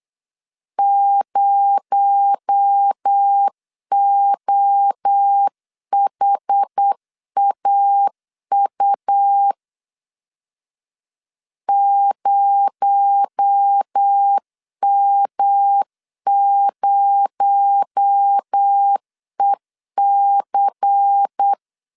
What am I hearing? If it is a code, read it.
Morse code, heard as a keyed tone, about 8 wpm, 800 Hz. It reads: 0OHAU 0M0EC
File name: LEZIONI / 6 PKX0 / 6 PKX0 8wpm 3341